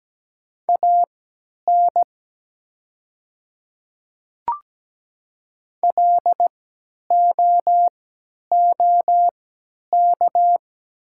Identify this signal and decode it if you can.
Morse code, heard as a keyed tone, about 17 wpm, spaced as for 10 wpm, 690 Hz. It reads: AN LOOK
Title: 3x Character Spacing - Top 100 Words - 17wpm